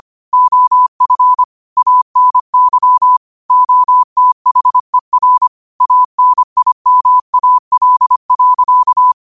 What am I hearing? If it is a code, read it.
Morse code, heard as a keyed tone, about 25 wpm, 1000 Hz. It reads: OF ANY OTHER ANIMAL.
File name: qotdfiles / 25wpm